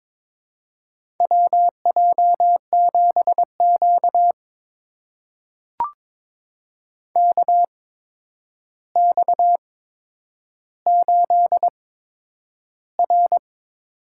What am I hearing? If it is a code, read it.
Morse code, heard as a keyed tone, about 21 wpm, spaced as for 9 wpm, 700 Hz. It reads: WJ7Q KX8R
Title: USA Call Signs - 4 Characters - 8x Character Spacing - 22wpm